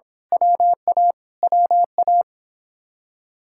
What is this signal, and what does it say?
Morse code, heard as a keyed tone, about 26 wpm, 700 Hz. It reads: WA WA
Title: Speed-Racing Common QSO Phrases - 22wpm